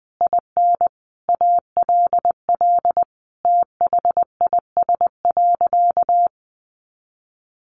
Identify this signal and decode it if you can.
Morse code, heard as a keyed tone, about 20 wpm, 700 Hz. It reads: IN ALL THIS.
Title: Sentences from Top 1400 Words 20wpm